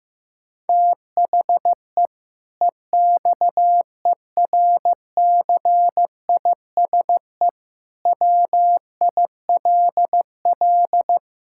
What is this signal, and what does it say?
Morse code, heard as a keyed tone, about 15 wpm, 700 Hz. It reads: THE EXERCISE WILL
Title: Sentences from Top 1000 Words - Review Part 6 - 15wpm